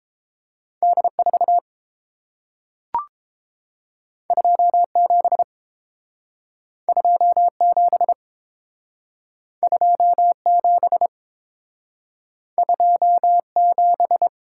Morse code, heard as a keyed tone, about 30 wpm, 700 Hz. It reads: D 4 2 7 2 7 2 7 2 7